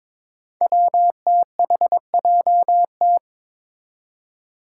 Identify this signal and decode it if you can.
Morse code, heard as a keyed tone, about 22 wpm, 700 Hz. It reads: WTHJT
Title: Five Challenging Letter-Number Combinations 22wpm